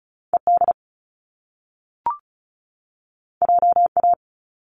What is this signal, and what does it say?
Morse code, heard as a keyed tone, about 36 wpm, 700 Hz. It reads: ED JA
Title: Top Two Letter Combinations in Words 35wpm